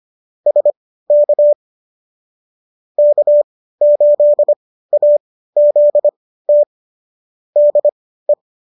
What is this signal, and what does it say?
Morse code, heard as a keyed tone, about 25 wpm, spaced as for 15 wpm, 590 Hz. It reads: SK K8AZT DE